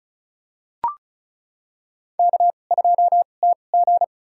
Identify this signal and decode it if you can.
Morse code, heard as a keyed tone, about 36 wpm, spaced as for 25 wpm, 700 Hz. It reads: K2TG
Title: USA Call Signs - 4 Characters - 2x Character Spacing - 35wpm